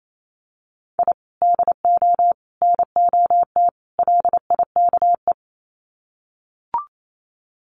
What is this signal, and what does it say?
Morse code, heard as a keyed tone, about 28 wpm, 700 Hz. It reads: I DO NOT LIKE
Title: Sets of 4 Words 28wpm